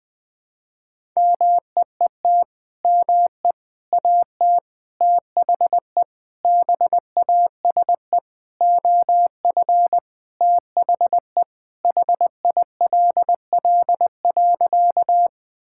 Morse code, heard as a keyed tone, about 20 wpm, 700 Hz. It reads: MEET ME AT THE BASE OF THE HILL.
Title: Sentences with 5-Word Phrases - Spoken Beginning - 20wpm